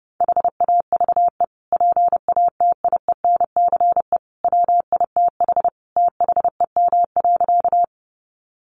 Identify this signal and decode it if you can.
Morse code, heard as a keyed tone, about 29 wpm, 700 Hz. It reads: HAVE PATIENCE WITH THEM.
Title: Sentences from Top 1300 Words 30wpm